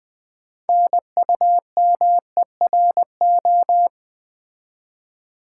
Morse code, heard as a keyed tone, about 21 wpm, 700 Hz. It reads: NUMERO
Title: Build A Word - 3 Letters At A Time - 8-Letter Words - 20wpm